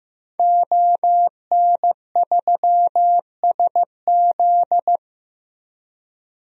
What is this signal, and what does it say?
Morse code, heard as a keyed tone, about 16 wpm, 700 Hz. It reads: ON3SZ
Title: DXCC-100 Call Signs - 5 Characters - Repeated 3x - 15wpm